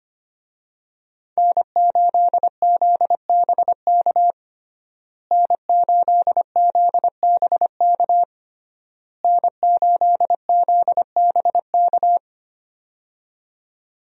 Morse code, heard as a keyed tone, about 25 wpm, 700 Hz. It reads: N8ZBK N8ZBK N8ZBK
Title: USA Call Signs - 5 Characters - Repeated 3x - 25wpm